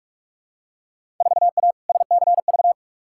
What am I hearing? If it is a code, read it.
Morse code, heard as a keyed tone, about 46 wpm, 700 Hz. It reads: 4A SCV
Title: Field Day Exchanges - 45wpm